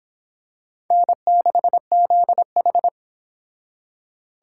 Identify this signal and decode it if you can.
Morse code, heard as a keyed tone, about 25 wpm, 700 Hz. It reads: N6ZH